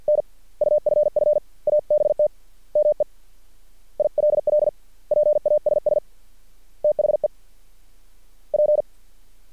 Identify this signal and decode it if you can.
Morse code, heard as a keyed tone, about 50 wpm, spaced as for 41 wpm, 600 Hz. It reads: NUFF ABT ME ILL PASS THE P
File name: archived / CRSnet